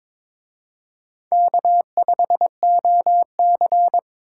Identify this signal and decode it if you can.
Morse code, heard as a keyed tone, about 26 wpm, 700 Hz. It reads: K5OC
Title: Speed Racing - USA 1x2 Call Signs 22wpm